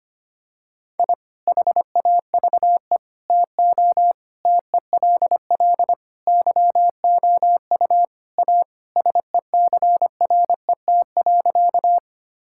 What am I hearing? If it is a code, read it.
Morse code, heard as a keyed tone, about 25 wpm, 700 Hz. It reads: I HAVE TO TELL YOU A SECRET.
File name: practice / 25wpm / Sentences with 3-Word Phrases - 25wpm